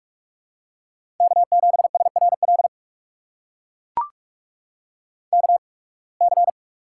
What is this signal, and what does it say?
Morse code, heard as a keyed tone, about 44 wpm, spaced as for 29 wpm, 700 Hz. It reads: K7SRL KC
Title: USA Call Signs - 8x Character Spacing - 45wpm